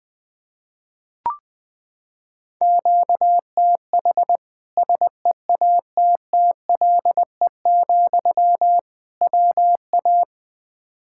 Morse code, heard as a keyed tone, about 20 wpm, 700 Hz. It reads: QTH SEATTLE, WA
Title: Practice QSO - Exchange 1 - Beginning - 20wpm